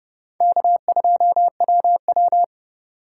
Morse code, heard as a keyed tone, about 32 wpm, 700 Hz. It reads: K2WW
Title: USA Call Signs - 8x Character Spacing - 30wpm